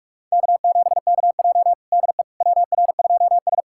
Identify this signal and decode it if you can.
Morse code, heard as a keyed tone, about 45 wpm, 700 Hz. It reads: K7KJ DE WR2S
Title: USA Callsign DE Callsign - 4 Characters - 45wpm